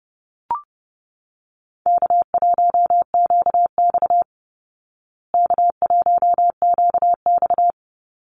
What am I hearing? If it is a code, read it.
Morse code, heard as a keyed tone, about 30 wpm, 700 Hz. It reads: K1QX K1QX
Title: Easy Call Signs Repeated 2x - 30wpm